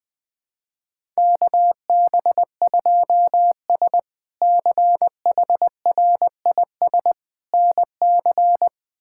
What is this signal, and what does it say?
Morse code, heard as a keyed tone, about 20 wpm, 700 Hz. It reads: KB2S CHRIS NC
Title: K1USN SST QSO Practice - 20wpm